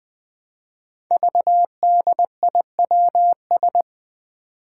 Morse code, heard as a keyed tone, about 20 wpm, 700 Hz. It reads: VDIWS
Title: Five Challenging Letter-Number Combinations 20wpm